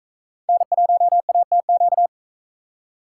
Morse code, heard as a keyed tone, about 43 wpm, 700 Hz. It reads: N1ATQ